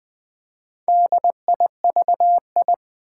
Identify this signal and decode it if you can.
Morse code, heard as a keyed tone, about 20 wpm, 700 Hz. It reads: DIVI
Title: Build A Word - 2 Letters At A Time - 6-Letter Words - 20wpm